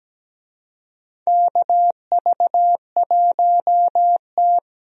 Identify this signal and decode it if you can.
Morse code, heard as a keyed tone, about 17 wpm, 700 Hz. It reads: KV1T
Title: USA Call Signs - 4 Characters - Repeated 3x - 17wpm